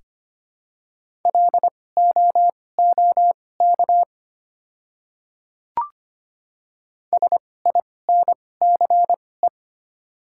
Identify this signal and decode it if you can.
Morse code, heard as a keyed tone, about 25 wpm, spaced as for 18 wpm, 700 Hz. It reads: LOOK SINCE
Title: 2x Character Spacing - Top 500 Words - 25wpm